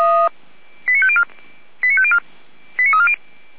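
Five-tone selective call (ZVEI), heard as quick runs of five tones, tones of about 70 ms. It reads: 89504 89504 89340
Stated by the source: ZVEI five-tone call received by SDR.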